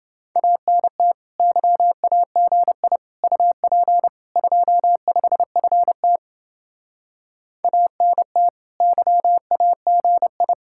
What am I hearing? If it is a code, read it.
Morse code, heard as a keyed tone, about 29 wpm, 700 Hz. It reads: ANT YAGI UP 25FT ANT YAGI